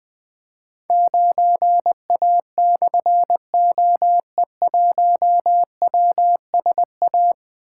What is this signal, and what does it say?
Morse code, heard as a keyed tone, about 20 wpm, 700 Hz. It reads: 9A/OE1WSA
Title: Difficult Call Signs 20wpm